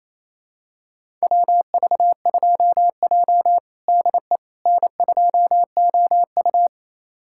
Morse code, heard as a keyed tone, about 27 wpm, 690 Hz. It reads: WV2J DE N2OU
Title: USA Callsign DE Callsign - 4 Characters - 28wpm